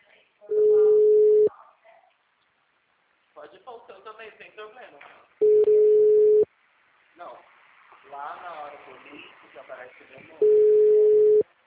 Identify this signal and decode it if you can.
Ringback tone (Europe, 425 Hz), with speech.